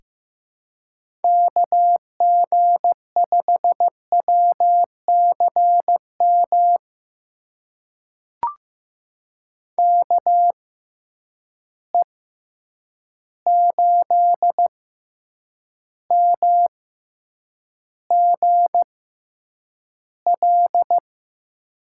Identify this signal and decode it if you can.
Morse code, heard as a keyed tone, about 15 wpm, spaced as for 8 wpm, 700 Hz. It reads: KG5WCM KE8MGL
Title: USA Call Signs - 6 Characters - 6x Character Spacing - 15wpm